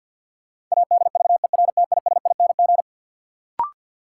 Morse code, heard as a keyed tone, about 53 wpm, 700 Hz. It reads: ADVERTISING
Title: Sentences from Top 1000 Words - Review Part 6 - New Word Drill - 50wpm